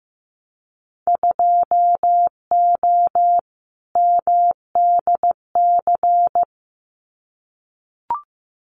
Morse code, heard as a keyed tone, about 15 wpm, 700 Hz. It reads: 2O MDC
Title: Winter Field Day Exchanges 15wpm